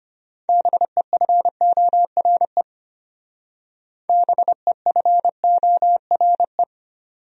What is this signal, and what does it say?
Morse code, heard as a keyed tone, about 32 wpm, 700 Hz. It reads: BEFORE BEFORE